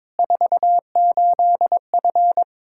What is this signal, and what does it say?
Morse code, heard as a keyed tone, about 21 wpm, 700 Hz. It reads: 48F
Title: Practice QSO - Exchange 2 - Info - 22wpm